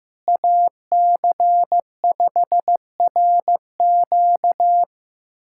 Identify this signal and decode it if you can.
Morse code, heard as a keyed tone, about 16 wpm, 700 Hz. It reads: AC5RQ
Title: USA Callsign DE Callsign - 5 Characters - 15wpm